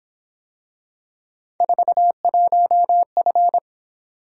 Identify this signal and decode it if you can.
Morse code, heard as a keyed tone, about 26 wpm, 700 Hz. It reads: 41F
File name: Speed-Racing - Three Letter-Number Combinations - 22wpm